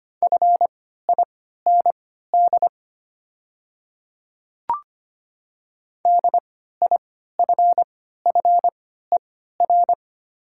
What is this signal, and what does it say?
Morse code, heard as a keyed tone, about 25 wpm, spaced as for 14 wpm, 700 Hz. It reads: FIND DIFFER
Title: 3x Character Spacing - Top 200 Words - 25wpm